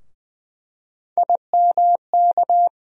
Morse code, heard as a keyed tone, about 20 wpm, 700 Hz. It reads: IMK
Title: USA Call Sign Postfixes - 3 Letters 20wpm